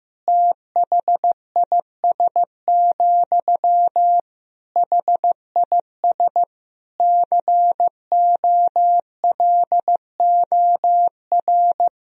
Morse code, heard as a keyed tone, about 15 wpm, 700 Hz. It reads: THIS, HIS COLOR
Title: Sentences from Top 1000 Words - Review Part 2 - 15wpm